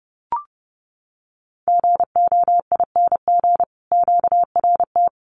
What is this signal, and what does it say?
Morse code, heard as a keyed tone, about 30 wpm, 700 Hz. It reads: GOING QRT
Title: Common QSO Phrases 30wpm